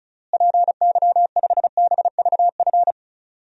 Morse code, heard as a keyed tone, about 34 wpm, 700 Hz. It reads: PY5BVF
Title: DXCC-100 Call Signs - 6 Characters - Repeated 3x - 35wpm